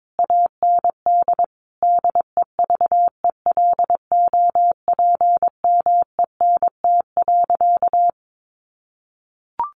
Morse code, heard as a keyed tone, about 22 wpm, 700 Hz. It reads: AND DEVELOPMENT.
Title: Sentences from Top 1400 Words 22wpm